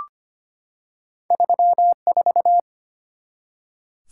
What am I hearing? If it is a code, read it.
Morse code, heard as a keyed tone, about 25 wpm, 700 Hz. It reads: 34
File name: Two Letter-Number Combinations - 25wpm